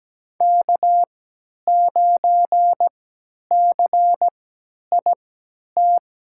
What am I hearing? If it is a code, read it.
Morse code, heard as a keyed tone, about 17 wpm, spaced as for 10 wpm, 700 Hz. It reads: K9CIT
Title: USA Call Signs - 5 Characters - 3x Character Spacing - 17wpm